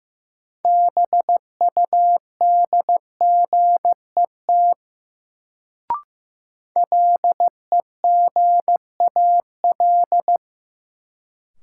Morse code, heard as a keyed tone, about 15 wpm, 700 Hz. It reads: BUDGET LEGAL
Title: Sentences from Top 1200 Words - New Word Drill - 15wpm